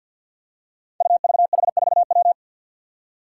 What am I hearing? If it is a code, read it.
Morse code, heard as a keyed tone, about 47 wpm, 700 Hz. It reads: UVH4W
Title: Five Challenging Letter-Number Combinations 50wpm